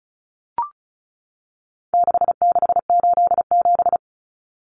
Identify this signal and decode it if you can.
Morse code, heard as a keyed tone, about 34 wpm, 700 Hz. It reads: 6687
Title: Common Cut Numbers - 4-Digit 35wpm